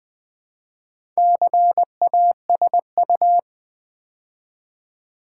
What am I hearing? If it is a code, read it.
Morse code, heard as a keyed tone, about 20 wpm, 700 Hz. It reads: CASU
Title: Build A Word - 2 Letters At A Time - 6-Letter Words - 20wpm